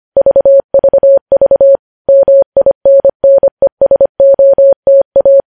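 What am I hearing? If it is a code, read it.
Morse code, heard as a keyed tone, about 25 wpm, 550 Hz. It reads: VVV MINNESOTA
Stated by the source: Morse code news headlines at 25 wpm.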